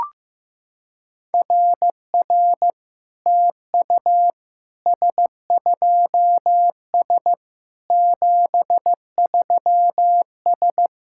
Morse code, heard as a keyed tone, about 15 wpm, 700 Hz. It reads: RR TU S2S 73S
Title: SOTA QSO Practice 15wpm